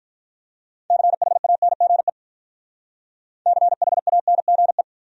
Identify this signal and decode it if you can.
Morse code, heard as a keyed tone, about 44 wpm, 700 Hz. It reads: CHANGE CHANGE
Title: Speed-Racing Top 200 Words 35wpm